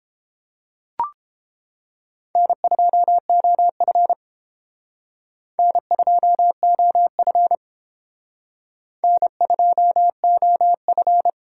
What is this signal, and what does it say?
Morse code, heard as a keyed tone, about 30 wpm, 700 Hz. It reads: N2OF N2OF N2OF